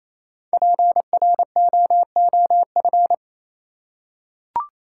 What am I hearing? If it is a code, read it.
Morse code, heard as a keyed tone, about 28 wpm, 700 Hz. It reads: PROOF
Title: Five Letter Words - 28wpm